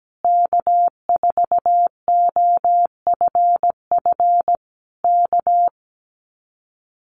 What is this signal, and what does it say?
Morse code, heard as a keyed tone, about 17 wpm, 700 Hz. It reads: K4OFF K
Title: Common QSO Phrases Encore 17wpm